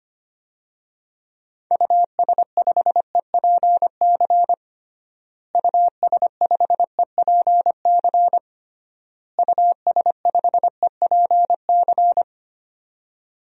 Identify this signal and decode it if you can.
Morse code, heard as a keyed tone, about 25 wpm, 700 Hz. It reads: US5EPC US5EPC US5EPC
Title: DXCC-100 Call Signs - 6 Characters - Repeated 3x - 25wpm